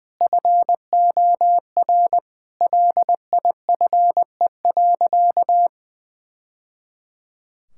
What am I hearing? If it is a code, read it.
Morse code, heard as a keyed tone, about 20 wpm, 700 Hz. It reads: FOR LIFE.
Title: Sentences from Top 1000 Words - Review Part 6 - 20wpm